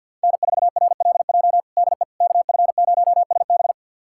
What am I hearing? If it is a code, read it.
Morse code, heard as a keyed tone, about 51 wpm, 690 Hz. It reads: N4RLJ DE KF0SB
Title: USA Callsign DE Callsign - 5 Characters - 50wpm